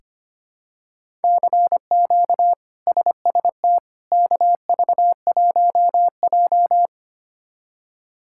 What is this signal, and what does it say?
Morse code, heard as a keyed tone, about 25 wpm, 700 Hz. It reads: CQ SST KV1J
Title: K1USN SST QSO Practice - 25wpm